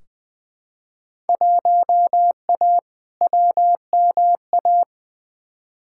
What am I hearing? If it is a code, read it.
Morse code, heard as a keyed tone, about 20 wpm, 700 Hz. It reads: 1A WMA
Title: Field Day Exchanges - 20wpm